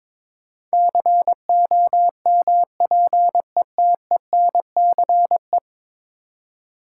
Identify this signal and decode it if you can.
Morse code, heard as a keyed tone, about 21 wpm, 700 Hz. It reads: COMPETENCE
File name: practice / 22wpm / 10 Letter Words - First 3 Letters Given - 22wpm